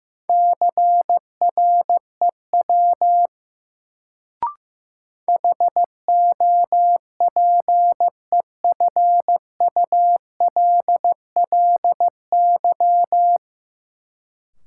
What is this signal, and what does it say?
Morse code, heard as a keyed tone, about 16 wpm, 700 Hz. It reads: CREW HOPEFULLY
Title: Sentences from Top 1500 Words - New Word Drill Alt - 15wpm